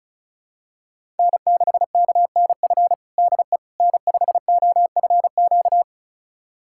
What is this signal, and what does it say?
Morse code, heard as a keyed tone, about 34 wpm, 700 Hz. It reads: N6KNF DE N5OFQ
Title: USA Callsign DE Callsign - 5 Characters - 35wpm